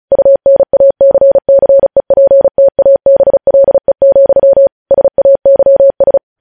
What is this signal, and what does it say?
Morse code, heard as a keyed tone, about 36 wpm, 550 Hz. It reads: UNACCEPTABLE, SAYS
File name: podcast / News headlines-1x-35x35